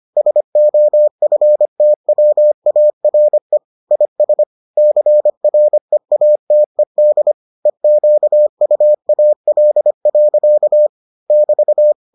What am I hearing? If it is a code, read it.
Morse code, heard as a keyed tone, about 25 wpm, 600 Hz. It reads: SOFTWARE IS CREATED EQUAL. <BT>